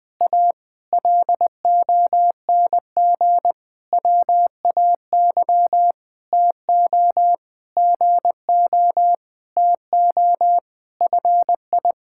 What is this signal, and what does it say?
Morse code, heard as a keyed tone, about 21 wpm, 700 Hz. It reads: A LONG WAY TO GO TO FI
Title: Sentences with 8-Word Phrases - Spoken Beginning - 20wpm